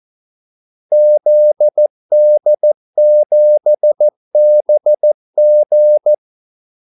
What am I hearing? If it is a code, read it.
Morse code, heard as a keyed tone, about 14 wpm, 600 Hz. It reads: ZD7BG
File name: Callsigns-14wpm